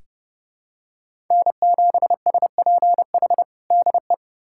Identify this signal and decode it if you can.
Morse code, heard as a keyed tone, about 30 wpm, 700 Hz. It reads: N7SPH DE
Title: USA Callsign DE Callsign - 5 Characters - 30wpm